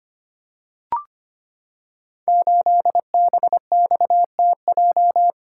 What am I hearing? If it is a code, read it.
Morse code, heard as a keyed tone, about 25 wpm, 700 Hz. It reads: 8BXTJ
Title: Code Groups - 5 Letter:Numbers - 25wpm